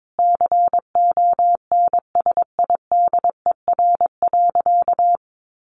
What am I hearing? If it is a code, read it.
Morse code, heard as a keyed tone, about 22 wpm, 700 Hz. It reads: CONSIDER.
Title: Sentences from Top 900 Words 22wpm